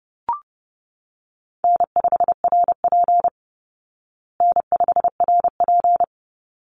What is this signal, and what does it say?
Morse code, heard as a keyed tone, about 28 wpm, 700 Hz. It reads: N5RP N5RP
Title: Call Signs Repeated 3x - 30wpm